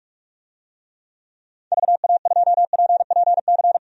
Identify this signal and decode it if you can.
Morse code, heard as a keyed tone, about 44 wpm, 700 Hz. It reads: VA2PPC
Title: DXCC-100 Call Signs - 6 Characters - 45wpm